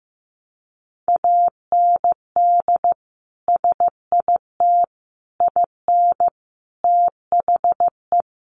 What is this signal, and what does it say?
Morse code, heard as a keyed tone, about 14 wpm, 700 Hz. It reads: AND SIT IN THE
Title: Sentences from Top 500 Words 15wpm